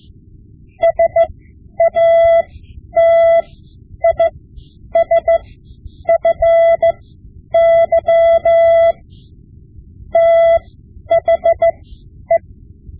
Morse code, heard as a keyed tone, about 15 wpm, spaced as for 9 wpm, 670 Hz. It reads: SATISFY THE
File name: Morse Lessons / Morse Lessons / Tape-5B